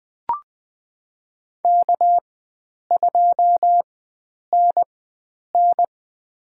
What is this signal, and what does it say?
Morse code, heard as a keyed tone, about 20 wpm, spaced as for 9 wpm, 700 Hz. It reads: K2NN
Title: USA Call Signs - 4 Characters - 4x Character Spacing - 20wpm